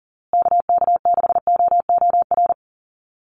Morse code, heard as a keyed tone, about 40 wpm, 700 Hz. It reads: KK6OOR
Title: USA Call Signs Repeated 3x 40wpm